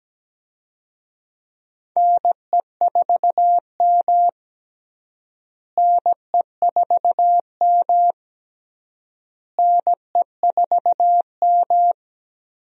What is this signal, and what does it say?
Morse code, heard as a keyed tone, about 17 wpm, 700 Hz. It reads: NE4M NE4M NE4M
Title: USA Call Signs - 4 Characters - Repeated 3x - 17wpm